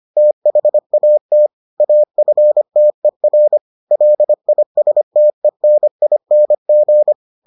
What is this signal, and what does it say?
Morse code, heard as a keyed tone, about 25 wpm, 600 Hz. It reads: THAT AFTER LISTENING